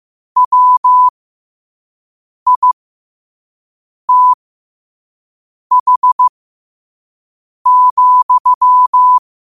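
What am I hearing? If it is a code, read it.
Morse code, heard as a keyed tone, about 15 wpm, spaced as for 5 wpm, 1000 Hz. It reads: WITH,
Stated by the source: 5 wpm Morse code practice recording.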